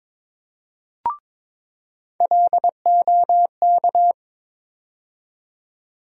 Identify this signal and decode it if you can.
Morse code, heard as a keyed tone, about 22 wpm, 700 Hz. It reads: LOK
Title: USA Call Sign Postfixes - 3 Letters 22wpm